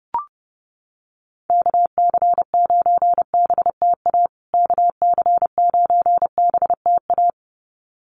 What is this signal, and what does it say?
Morse code, heard as a keyed tone, about 28 wpm, 700 Hz. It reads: KC9BTA KC9BTA
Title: USA Call Signs Repeated 2x 30wpm